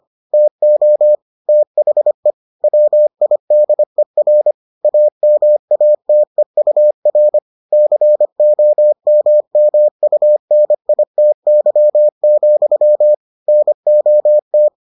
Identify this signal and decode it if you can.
Morse code, heard as a keyed tone, about 25 wpm, 600 Hz. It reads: TO THE WIDER AMATEUR COMMUNITY, NOT